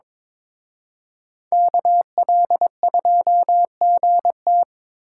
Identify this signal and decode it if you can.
Morse code, heard as a keyed tone, about 27 wpm, 700 Hz. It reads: KL2GT
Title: Speed Racing - USA Call Signs 22wpm